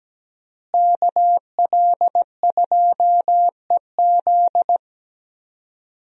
Morse code, heard as a keyed tone, about 17 wpm, 700 Hz. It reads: KL2EZ
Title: USA Call Signs - 5 Characters - Repeated 3x - 17wpm